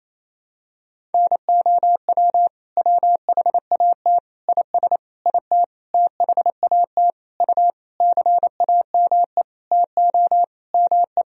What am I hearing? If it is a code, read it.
Morse code, heard as a keyed tone, about 28 wpm, 700 Hz. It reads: NOW WHAT IS IT THAT U CAME TO ME
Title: Sentences from Top 200 Words and Abbreviations 28wpm